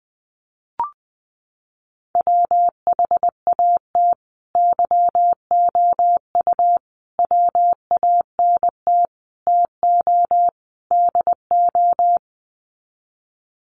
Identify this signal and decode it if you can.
Morse code, heard as a keyed tone, about 20 wpm, 700 Hz. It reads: WHAT YOU WANT TO DO
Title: Sets of 5 Words Encore 20wpm